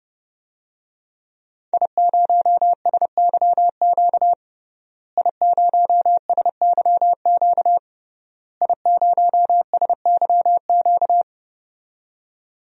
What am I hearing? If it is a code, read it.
Morse code, heard as a keyed tone, about 30 wpm, 700 Hz. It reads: I0SYQ I0SYQ I0SYQ
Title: DXCC-100 Call Signs - 5 Characters - Repeated 3x - 30wpm